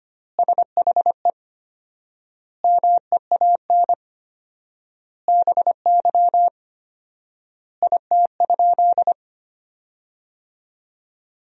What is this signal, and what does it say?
Morse code, heard as a keyed tone, about 25 wpm, 700 Hz. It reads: SHE MEAN BY IT?
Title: 4x Word-Spacing - Sentences from Top 200 Words - 25wpm